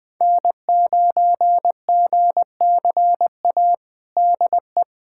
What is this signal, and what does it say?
Morse code, heard as a keyed tone, about 20 wpm, 700 Hz. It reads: N9GCA DE
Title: USA Callsign DE Callsign - 5 Characters - 20wpm